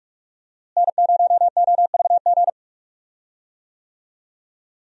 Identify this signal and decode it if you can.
Morse code, heard as a keyed tone, about 46 wpm, 700 Hz. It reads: N0OVG
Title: USA Call Signs - 5 Characters - Repeated 3x - 45wpm